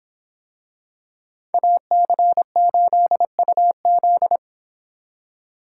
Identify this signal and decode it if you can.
Morse code, heard as a keyed tone, about 26 wpm, 700 Hz. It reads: AC8UZ